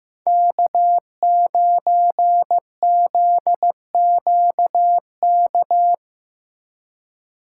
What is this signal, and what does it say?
Morse code, heard as a keyed tone, about 17 wpm, 700 Hz. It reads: K9ZQK
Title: Speed Racing - USA 1x3 Call Signs 15wpm